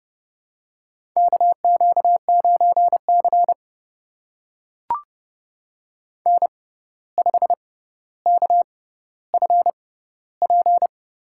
Morse code, heard as a keyed tone, about 30 wpm, spaced as for 14 wpm, 700 Hz. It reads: KQ9C N5KFP
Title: USA Call Signs - 6x Character Spacing - 30wpm